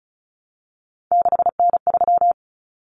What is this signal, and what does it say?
Morse code, heard as a keyed tone, about 34 wpm, 700 Hz. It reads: 6N3
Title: Common Cut Numbers - 3-Digit 35wpm